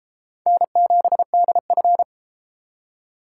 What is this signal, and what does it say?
Morse code, heard as a keyed tone, about 32 wpm, 700 Hz. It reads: N7DF